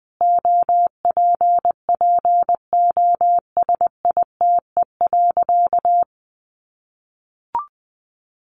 Morse code, heard as a keyed tone, about 20 wpm, 700 Hz. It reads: OPPOSITE.
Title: Sentences from Top 1300 Words 20wpm